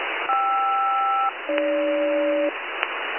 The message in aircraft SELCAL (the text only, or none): KR-AG